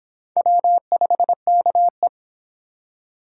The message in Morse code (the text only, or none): W5KE